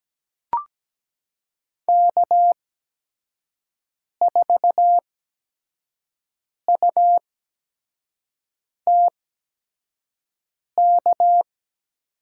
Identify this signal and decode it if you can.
Morse code, heard as a keyed tone, about 17 wpm, spaced as for 5 wpm, 700 Hz. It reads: K4UTK
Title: USA Call Signs - 5 Characters - 8x Character Spacing - 17wpm